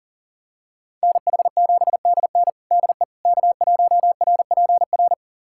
Morse code, heard as a keyed tone, about 40 wpm, 700 Hz. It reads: NH7DN DE K1RPR
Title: USA Callsign DE Callsign 40wpm